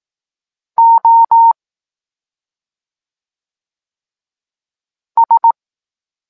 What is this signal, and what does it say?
Morse code, heard as a keyed tone, about 18 wpm, 920 Hz. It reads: O S
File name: Tekst-9-rownomierny